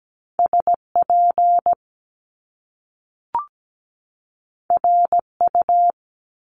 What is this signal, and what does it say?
Morse code, heard as a keyed tone, about 17 wpm, 700 Hz. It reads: SP RU
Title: Top Two Letter Combinations in Words 17wpm